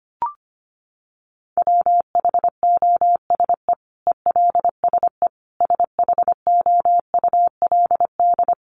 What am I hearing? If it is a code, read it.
Morse code, heard as a keyed tone, about 25 wpm, 700 Hz. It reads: WHOSE ELSE SHOULD
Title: Sentences from Top 1200 Words 25wpm